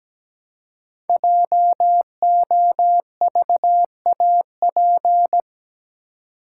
Morse code, heard as a keyed tone, about 17 wpm, 700 Hz. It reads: JOVAP
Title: Code Groups - 5 Letters - 17wpm